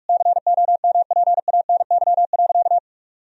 Morse code, heard as a keyed tone, about 46 wpm, 700 Hz. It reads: COMPANY.
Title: Sentences from Top 1000 Words - Review Part 3 - 45wpm